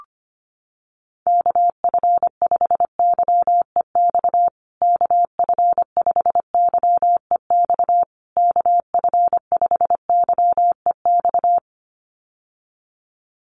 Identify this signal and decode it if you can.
Morse code, heard as a keyed tone, about 25 wpm, 700 Hz. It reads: KF5YEX KF5YEX KF5YEX
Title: USA Call Signs Repeated 3x 25wpm